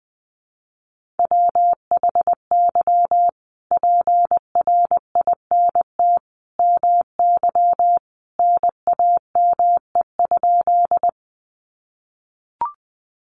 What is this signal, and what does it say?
Morse code, heard as a keyed tone, about 20 wpm, 700 Hz. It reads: WHY PRINT MY NAME?
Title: Sentences from Top 1400 Words 20wpm